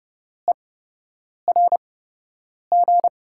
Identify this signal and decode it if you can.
Morse code, heard as a keyed tone, about 30 wpm, spaced as for 8 wpm, 700 Hz. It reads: ERG
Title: USA Call Signs - 6 Characters - 8x Character Spacing - 30wpm